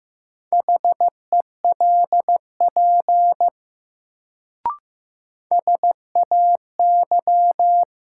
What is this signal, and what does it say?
Morse code, heard as a keyed tone, about 15 wpm, 700 Hz. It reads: HELP SAY
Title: Sets of 2 Words - New Word Drill - 15wpm